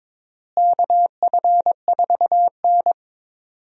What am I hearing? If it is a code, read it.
Morse code, heard as a keyed tone, about 21 wpm, 700 Hz. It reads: KF4N
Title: USA Call Signs - 4 Characters - Repeated 2x - 22wpm